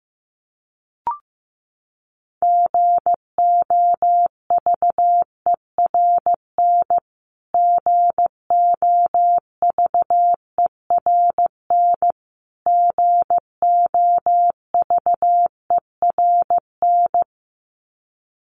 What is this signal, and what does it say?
Morse code, heard as a keyed tone, about 15 wpm, 700 Hz. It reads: GOVERN GOVERN GOVERN
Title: Top 500 Words Repeated 15wpm